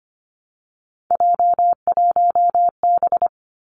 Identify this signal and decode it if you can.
Morse code, heard as a keyed tone, about 25 wpm, 700 Hz. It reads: J1B
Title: Three Challenging Letter-Number Combinations 25wpm